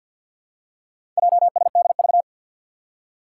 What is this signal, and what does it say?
Morse code, heard as a keyed tone, about 46 wpm, 700 Hz. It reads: JSDV
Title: Four Challenging Letter-Number Combinations 50wpm